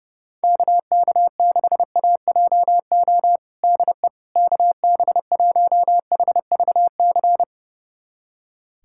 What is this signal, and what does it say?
Morse code, heard as a keyed tone, about 32 wpm, 700 Hz. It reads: KK6AJO DE KB1HVC
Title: USA Callsign DE Callsign 30wpm